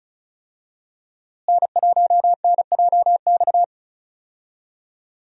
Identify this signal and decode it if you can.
Morse code, heard as a keyed tone, about 34 wpm, 700 Hz. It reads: N1NJX
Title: USA Call Signs - 5 Characters - Repeated 2x - 35wpm